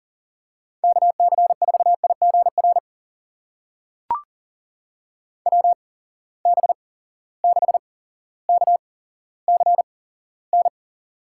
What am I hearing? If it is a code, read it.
Morse code, heard as a keyed tone, about 40 wpm, spaced as for 17 wpm, 700 Hz. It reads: KC4IGR WB6KCN
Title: USA Call Signs - 6 Characters - 8x Character Spacing - 40wpm